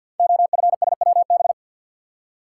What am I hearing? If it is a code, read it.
Morse code, heard as a keyed tone, about 50 wpm, 700 Hz. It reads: OFSWB
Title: Code Groups - 5 Letters - 50wpm